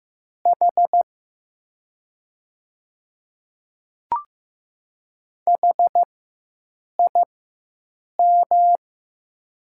Morse code, heard as a keyed tone, about 15 wpm, spaced as for 7 wpm, 700 Hz. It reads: H HIM
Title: 4x Character Spacing - Top 100 Words - 15wpm